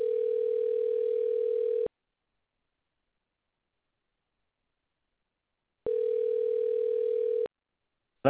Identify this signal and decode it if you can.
Ringback tone (North America), 440 + 480 Hz.